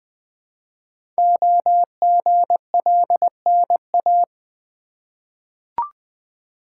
Morse code, heard as a keyed tone, about 20 wpm, 700 Hz. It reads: OGLNA
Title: Code Groups - 5 Letters - 20wpm